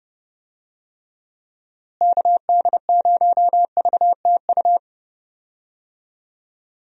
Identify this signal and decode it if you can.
Morse code, heard as a keyed tone, about 30 wpm, 700 Hz. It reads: KD0VTU
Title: USA Call Signs - 6 Characters - Repeated 3x - 30wpm